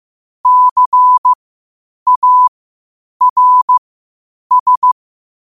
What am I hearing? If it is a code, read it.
Morse code, heard as a keyed tone, about 15 wpm, spaced as for 8 wpm, 1000 Hz. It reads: CARS
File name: qotdfiles / 8wpm